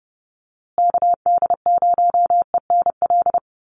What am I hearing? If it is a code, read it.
Morse code, heard as a keyed tone, about 30 wpm, 700 Hz. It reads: KD0ENL
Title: USA Call Signs Repeated 3x 30wpm